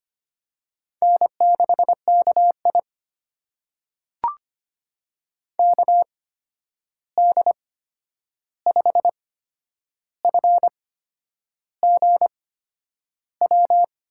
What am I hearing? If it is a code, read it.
Morse code, heard as a keyed tone, about 25 wpm, spaced as for 9 wpm, 700 Hz. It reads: N6KI KD5FGW
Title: USA Call Signs - 8x Character Spacing - 25wpm